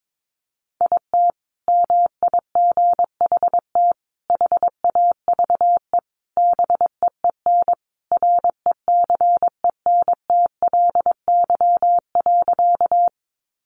Morse code, heard as a keyed tone, about 22 wpm, 700 Hz. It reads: IT MIGHT HAVE BEEN RECENTLY.
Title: Sentences from Top 1200 Words 22wpm